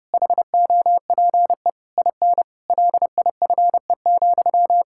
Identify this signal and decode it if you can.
Morse code, heard as a keyed tone, about 30 wpm, 700 Hz. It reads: HOPE IN LIFE,
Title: Sentences from Top 1000 Words - Review Part 6 - 30wpm